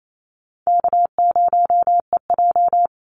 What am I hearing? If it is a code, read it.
Morse code, heard as a keyed tone, about 28 wpm, 700 Hz. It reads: K0EJ
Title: Call Signs - 28wpm